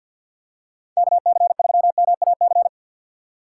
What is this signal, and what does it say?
Morse code, heard as a keyed tone, about 48 wpm, 700 Hz. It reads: KC3MAC
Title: USA Call Signs - 6 Characters - 6x Character Spacing - 50wpm